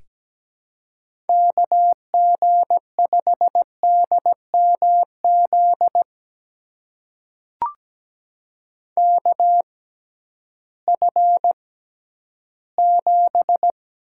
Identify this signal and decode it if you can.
Morse code, heard as a keyed tone, about 17 wpm, spaced as for 11 wpm, 700 Hz. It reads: KG5DMZ KF7
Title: USA Call Signs - 6 Characters - 6x Character Spacing - 17wpm